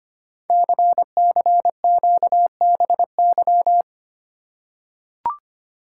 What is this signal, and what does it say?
Morse code, heard as a keyed tone, about 26 wpm, 700 Hz. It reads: CCQBY
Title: Code Groups - 5 Letters - 25wpm